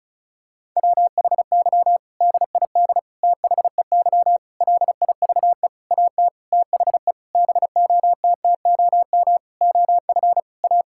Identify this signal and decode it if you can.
Morse code, heard as a keyed tone, about 34 wpm, 700 Hz. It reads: WHY DID THEY LIVE AT THE BOTTOM OF A
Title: Sentences from Top 1000 Words - Review Part 6 - 35wpm